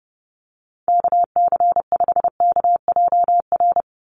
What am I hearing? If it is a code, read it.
Morse code, heard as a keyed tone, about 30 wpm, 700 Hz. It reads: KC5KJR
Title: USA Call Signs 30wpm